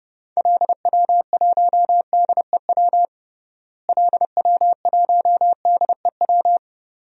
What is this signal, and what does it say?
Morse code, heard as a keyed tone, about 30 wpm, 700 Hz. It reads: LW1DEW LW1DEW
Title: DXCC-100 Call Signs - 6 Characters - Repeated 2x - 30wpm